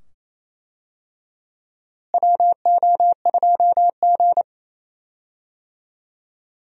Morse code, heard as a keyed tone, about 28 wpm, 700 Hz. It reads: WO2G
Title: USA Call Signs - 4 Characters - Repeated 2x - 28wpm